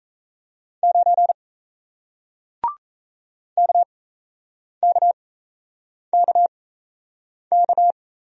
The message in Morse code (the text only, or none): 9 KKKK